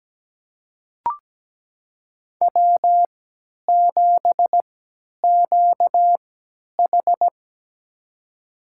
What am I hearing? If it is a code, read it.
Morse code, heard as a keyed tone, about 17 wpm, spaced as for 10 wpm, 700 Hz. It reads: W7QH
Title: USA Call Signs - 3x Character Spacing - 17wpm